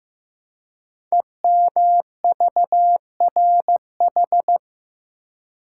Morse code, heard as a keyed tone, about 15 wpm, 700 Hz. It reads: EMVRH
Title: Code Groups - 5 Letters - 15wpm